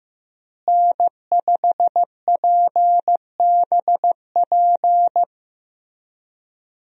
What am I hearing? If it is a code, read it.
Morse code, heard as a keyed tone, about 15 wpm, 700 Hz. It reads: N5PBP
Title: USA Call Signs - 5 Characters - Repeated 3x - 15wpm